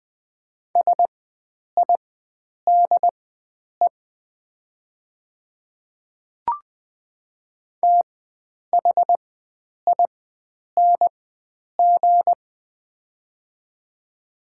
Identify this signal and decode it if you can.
Morse code, heard as a keyed tone, about 20 wpm, spaced as for 9 wpm, 700 Hz. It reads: SIDE THING
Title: 4x Character Spacing - Top 100 Words - 20wpm